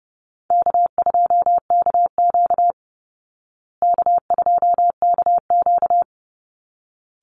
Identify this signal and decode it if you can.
Morse code, heard as a keyed tone, about 30 wpm, 700 Hz. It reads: K2KQ K2KQ
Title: Call Signs Repeated 3x - 30wpm